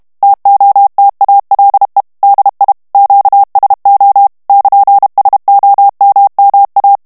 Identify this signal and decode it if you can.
Morse code, heard as a keyed tone, about 32 wpm, 800 Hz. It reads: TOTALE DI QSO <KN>SOMMA
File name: ITALIEN texte clair 2 vitesse 30 mots minute